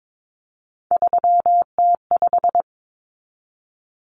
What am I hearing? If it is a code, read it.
Morse code, heard as a keyed tone, about 22 wpm, 700 Hz. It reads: 3T5
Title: Common Cut Numbers - 3-Digit 22wpm